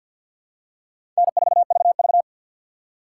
Morse code, heard as a keyed tone, about 47 wpm, 700 Hz. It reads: N4VV
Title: USA Call Signs - 2x Character Spacing - 50wpm